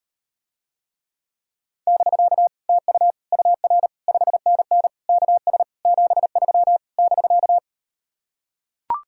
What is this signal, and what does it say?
Morse code, heard as a keyed tone, about 38 wpm, 700 Hz. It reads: <BK> TU UR 5NN KS 73 <BK>